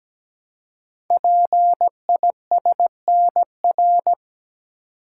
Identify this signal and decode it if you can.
Morse code, heard as a keyed tone, about 17 wpm, 700 Hz. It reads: PISNR
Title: Code Groups - 5 Letters - 17wpm